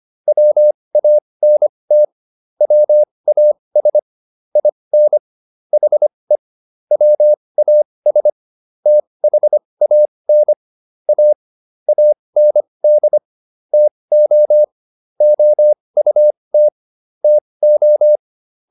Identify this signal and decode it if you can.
Morse code, heard as a keyed tone, about 25 wpm, spaced as for 20 wpm, 600 Hz. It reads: WANT WAS IN HE WAS THAN A AND TO OUT TO